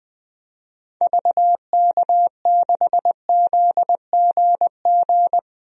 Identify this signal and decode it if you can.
Morse code, heard as a keyed tone, about 21 wpm, 700 Hz. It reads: VK6ZGG
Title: DXCC-100 Call Signs - 6 Characters - Repeated 2x - 20wpm